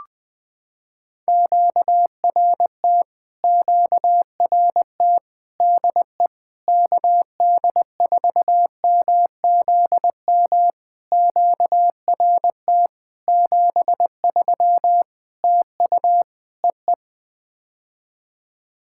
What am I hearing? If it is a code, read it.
Morse code, heard as a keyed tone, about 20 wpm, 700 Hz. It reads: QRT QRT DE KD4MZM QRT 73 TU EE
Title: POTA QSO Practice - 20wpm